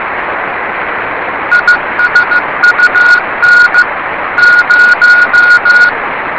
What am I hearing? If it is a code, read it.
Morse code, heard as a keyed tone, about 16 wpm, 1440 Hz, with radio noise. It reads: ISUN 0
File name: rs30 cw